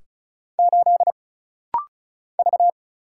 Morse code, heard as a keyed tone, about 35 wpm, 700 Hz. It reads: 8 V
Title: Single Letter-Number Alt - Rapid-Fire 35wpm